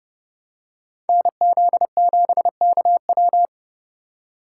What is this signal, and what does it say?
Morse code, heard as a keyed tone, about 30 wpm, 700 Hz. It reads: NZ7KW